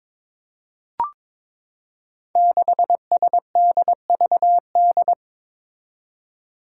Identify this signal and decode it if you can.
Morse code, heard as a keyed tone, about 22 wpm, 700 Hz. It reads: 6SDVD
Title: Five Challenging Letter-Number Combinations 22wpm